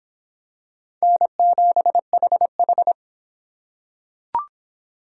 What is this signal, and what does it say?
Morse code, heard as a keyed tone, about 23 wpm, 700 Hz. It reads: N7HH